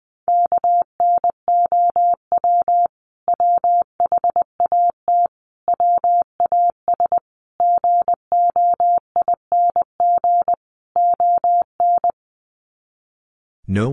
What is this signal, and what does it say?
Morse code, heard as a keyed tone, about 20 wpm, 700 Hz. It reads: KNOW WHAT WAS GOING ON
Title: Sets of 5 Words Encore2 20wpm